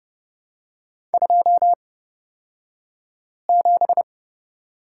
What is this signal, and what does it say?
Morse code, heard as a keyed tone, about 30 wpm, spaced as for 5 wpm, 700 Hz. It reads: 27
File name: Lesson 37d - Letter x - Two Characters 30wpm